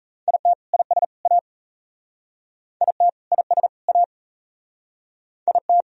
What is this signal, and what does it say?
Morse code, heard as a keyed tone, about 35 wpm, 700 Hz. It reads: IT IS A IT IS A IT